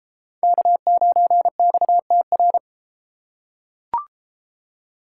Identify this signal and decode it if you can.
Morse code, heard as a keyed tone, about 31 wpm, 700 Hz. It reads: K9XTR